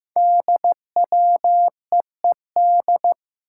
Morse code, heard as a keyed tone, about 15 wpm, 700 Hz. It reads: DWEED
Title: Code Groups - 5 Letter:Numbers - 15wpm